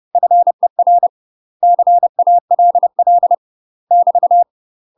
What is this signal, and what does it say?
Morse code, heard as a keyed tone, about 30 wpm, 700 Hz. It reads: FER CALL <BT>
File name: UK MASTER QSOs TRAINING / 30WPM / UK MASTER QSOs 0005